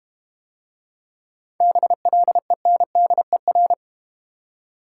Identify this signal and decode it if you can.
Morse code, heard as a keyed tone, about 32 wpm, 700 Hz. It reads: BLENDER